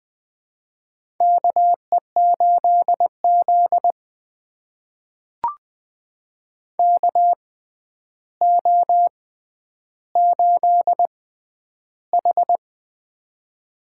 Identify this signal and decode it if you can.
Morse code, heard as a keyed tone, about 20 wpm, spaced as for 10 wpm, 700 Hz. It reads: KE8Z KO8H
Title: USA Call Signs - 4 Characters - 6x Character Spacing - 20wpm